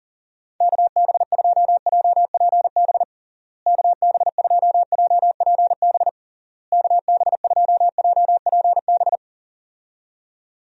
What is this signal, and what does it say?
Morse code, heard as a keyed tone, about 37 wpm, 700 Hz. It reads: KB2JPB KB2JPB KB2JPB
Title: USA Call Signs - 6 Characters - Repeated 3x - 40wpm